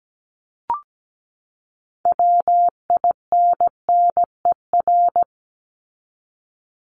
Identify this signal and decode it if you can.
Morse code, heard as a keyed tone, about 17 wpm, 700 Hz. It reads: WINNER
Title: Top 1751-2000 Words 17wpm